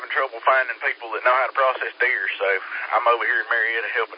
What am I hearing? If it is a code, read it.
Emergency Alert System (SAME) tones, heard as harsh buzzing bursts, with speech. It is heard from 0.6 to 3.6 s.